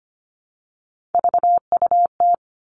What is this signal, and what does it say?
Morse code, heard as a keyed tone, about 25 wpm, 700 Hz. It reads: VUT
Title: Three Challenging Letter-Number Combinations 25wpm